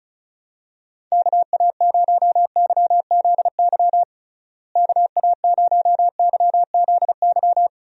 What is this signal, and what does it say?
Morse code, heard as a keyed tone, about 36 wpm, 700 Hz. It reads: KA0YZY KA0YZY
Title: USA Call Signs - 6 Characters - Repeated 3x - 35wpm